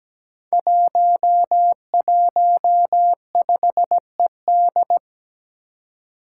Morse code, heard as a keyed tone, about 17 wpm, 700 Hz. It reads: 115ED
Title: Five Challenging Letter-Number Combinations 17wpm